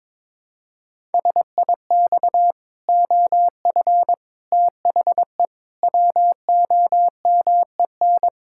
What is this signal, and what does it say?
Morse code, heard as a keyed tone, about 22 wpm, 700 Hz. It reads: SIX OF THE WOMEN
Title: Pangram Challenge v2 - 22wpm